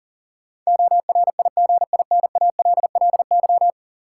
Morse code, heard as a keyed tone, about 40 wpm, 700 Hz. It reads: ORIGINALLY
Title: Sentences from Top 1500 Words - New Word Drill Alt - 40wpm